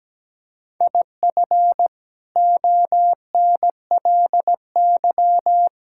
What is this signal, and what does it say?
Morse code, heard as a keyed tone, about 17 wpm, 700 Hz. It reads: IF ONLY
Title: Sentences from Top 1000 Words - Review Part 6 - 17wpm